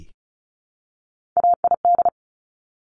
Morse code, heard as a keyed tone, about 36 wpm, 700 Hz. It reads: AID
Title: Top Three Letter Combinations in Words 35wpm